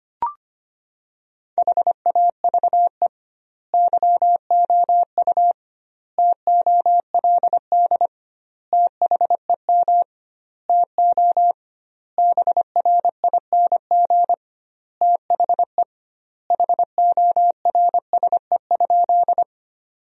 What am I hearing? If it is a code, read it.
Morse code, heard as a keyed tone, about 25 wpm, 700 Hz. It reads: HAVE YOU TOLD THEM TO BRING THE HORSE?
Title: 2x Word-Spacing - Sentences from Top 500 Words - 25wpm